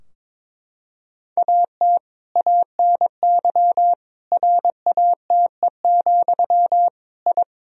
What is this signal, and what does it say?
Morse code, heard as a keyed tone, about 21 wpm, 700 Hz. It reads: AT ANY RATE, I
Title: Sentences from Top 1000 Words - Review Part 4 - 22wpm